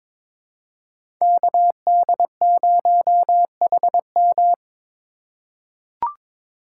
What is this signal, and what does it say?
Morse code, heard as a keyed tone, about 22 wpm, 700 Hz. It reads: KD0HM
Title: USA Call Signs - 5 Characters - 3x Character Spacing - 22wpm